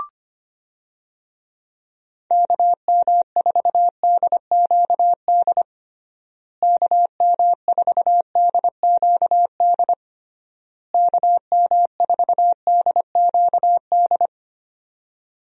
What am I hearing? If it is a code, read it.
Morse code, heard as a keyed tone, about 25 wpm, 700 Hz. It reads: KM4DQD KM4DQD KM4DQD
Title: USA Call Signs - 6 Characters - Repeated 3x - 25wpm